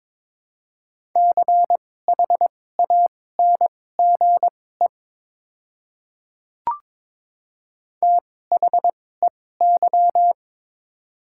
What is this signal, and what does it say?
Morse code, heard as a keyed tone, about 22 wpm, spaced as for 16 wpm, 700 Hz. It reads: CHANGE THEY
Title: 2x Character Spacing - Top 200 Words - 22wpm